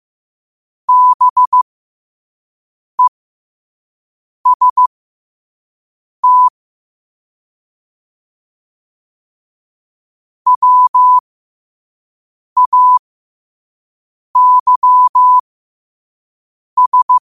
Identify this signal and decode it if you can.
Morse code, heard as a keyed tone, about 15 wpm, spaced as for 5 wpm, 1000 Hz. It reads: BEST WAYS
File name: qotdfiles / 5wpm